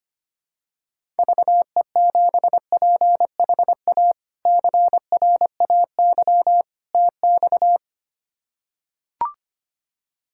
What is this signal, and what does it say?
Morse code, heard as a keyed tone, about 25 wpm, 700 Hz. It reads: VE7PHA CRAY TX
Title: K1USN SST QSO Practice - 25wpm